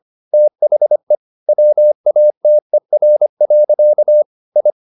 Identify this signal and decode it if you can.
Morse code, heard as a keyed tone, about 25 wpm, 600 Hz. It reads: THE WATER. I
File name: the old man and the sea part 1 25wpm